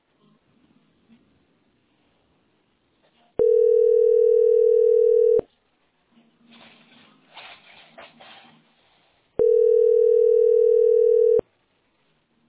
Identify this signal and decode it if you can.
Ringback tone (North America), 440 + 480 Hz.